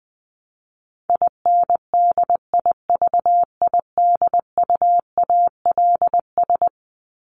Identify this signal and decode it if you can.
Morse code, heard as a keyed tone, about 20 wpm, 700 Hz. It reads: INDIVIDUALS
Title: ES:S Suffix - Top 100 Words with Base Words 20wpm